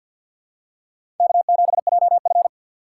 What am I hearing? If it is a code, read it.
Morse code, heard as a keyed tone, about 48 wpm, 700 Hz. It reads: K7JF
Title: USA Call Signs - 4 Characters - 6x Character Spacing - 50wpm